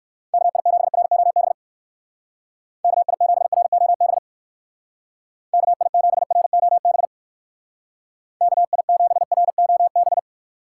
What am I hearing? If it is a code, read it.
Morse code, heard as a keyed tone, about 56 wpm, 700 Hz. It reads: KI7ROB KI7ROB KI7ROB KI7ROB
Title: Speed Racing - USA 2x3 Call Signs 45wpm